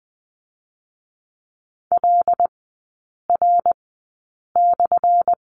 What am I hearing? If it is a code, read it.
Morse code, heard as a keyed tone, about 20 wpm, 700 Hz. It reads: L R /
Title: 212 Lesson 34e - Number 8 - Three Characters 20wpm